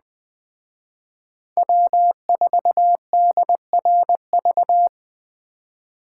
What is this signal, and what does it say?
Morse code, heard as a keyed tone, about 21 wpm, 700 Hz. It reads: W4DRV
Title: Speed Racing - USA 1x3 Call Signs 17wpm